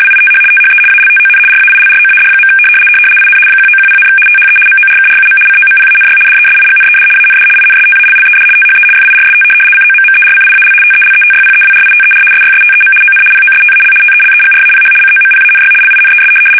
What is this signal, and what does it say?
RTTY, about 75 baud. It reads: WITH THE ALTON RAILROAD. IN 1910, THE MORKRUM COMPANY DESIGNED AND INSTALLED THE FIRST COMMERCIAL TELETYPEWRITER SYSTEM ON POSTAL TELEGRAPH COMPANY LINES
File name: fsk850-75B / teletype-corp